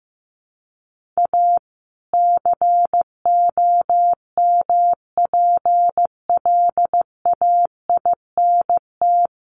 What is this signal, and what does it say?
Morse code, heard as a keyed tone, about 15 wpm, 700 Hz. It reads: A COMPLAINT
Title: Sentences from Top 1400 Words 15wpm